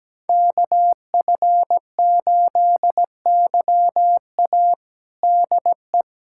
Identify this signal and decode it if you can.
Morse code, heard as a keyed tone, about 17 wpm, 700 Hz. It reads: KF8YA DE
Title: USA Callsign DE Callsign - 5 Characters - 17wpm